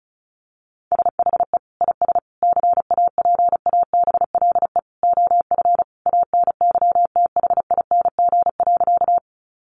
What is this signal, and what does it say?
Morse code, heard as a keyed tone, about 34 wpm, 700 Hz. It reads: SHE IS CAPABLE OF ANYTHING.
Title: Sentences from Top 1200 Words 35wpm